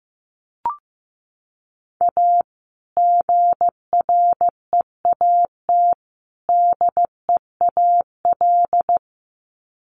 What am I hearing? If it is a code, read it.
Morse code, heard as a keyed tone, about 15 wpm, 700 Hz. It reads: A GREAT DEAL
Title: Sets of 3 Words 15wpm